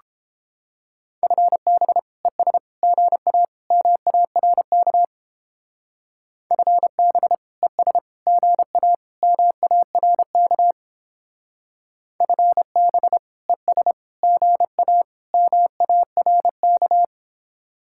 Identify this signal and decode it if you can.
Morse code, heard as a keyed tone, about 31 wpm, 700 Hz. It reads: FB ES GA MARK FB ES GA MARK FB ES GA MARK